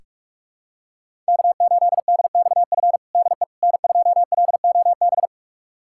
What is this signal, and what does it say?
Morse code, heard as a keyed tone, about 43 wpm, 700 Hz. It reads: K8DXF DE N2LOB
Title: USA Callsign DE Callsign - 5 Characters - 45wpm